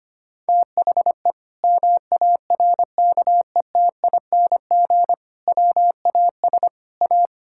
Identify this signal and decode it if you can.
Morse code, heard as a keyed tone, about 25 wpm, 700 Hz. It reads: THE MARKETING WAS A
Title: Sentences from Top 1000 Words - Review Part 5 - 25wpm